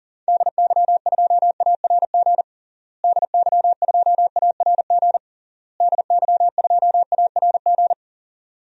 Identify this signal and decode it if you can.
Morse code, heard as a keyed tone, about 37 wpm, 700 Hz. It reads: DY2ARG DY2ARG DY2ARG
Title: DXCC-100 Call Signs - 6 Characters - Repeated 3x - 40wpm